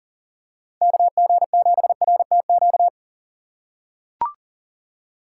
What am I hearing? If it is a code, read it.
Morse code, heard as a keyed tone, about 40 wpm, 700 Hz. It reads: KG7RTQ
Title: USA Call Signs - 6 Characters - 2x Character Spacing - 40wpm